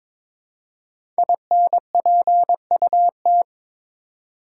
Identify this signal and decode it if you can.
Morse code, heard as a keyed tone, about 22 wpm, 700 Hz. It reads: INPUT
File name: 5 Letter Words - First 3 Letters Given - 22wpm